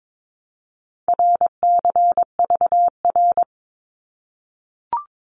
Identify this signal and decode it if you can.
Morse code, heard as a keyed tone, about 22 wpm, 700 Hz. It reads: RCVR
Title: Abbreviations 22wpm